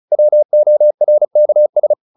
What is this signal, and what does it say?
Morse code, heard as a keyed tone, about 34 wpm, 600 Hz. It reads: WORKS